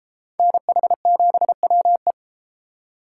Morse code, heard as a keyed tone, about 31 wpm, 700 Hz. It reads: NH7WE